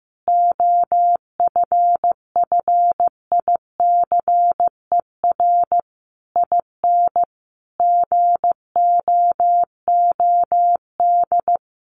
Morse code, heard as a keyed tone, about 15 wpm, 700 Hz. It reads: OFFICER IN GOOD
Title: Sentences from Top 1200 Words 15wpm